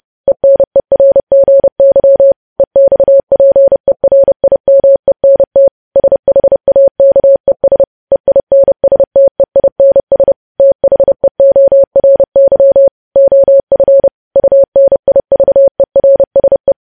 Morse code, heard as a keyed tone, about 30 wpm, 550 Hz. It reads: ENERGY EXPERIMENT SHAKES EINSTEINS THEORY OF UNIVERSE
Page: news headlines in morse code at 30 wpm